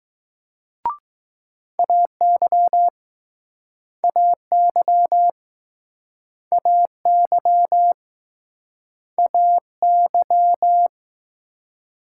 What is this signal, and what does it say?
Morse code, heard as a keyed tone, about 21 wpm, 700 Hz. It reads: AY AY AY AY